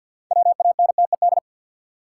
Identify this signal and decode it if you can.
Morse code, heard as a keyed tone, about 52 wpm, 700 Hz. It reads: WANTED
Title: Sets of 2 Words - New Word Drill - 50wpm